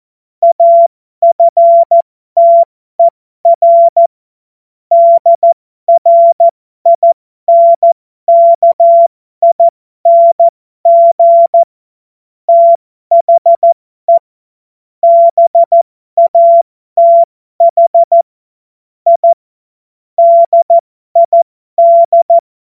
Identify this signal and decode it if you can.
Morse code, heard as a keyed tone, about 14 wpm, spaced as for 12 wpm, 680 Hz. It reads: AFTER DRINKING THE BATH I DID
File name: Morsekurs / Witze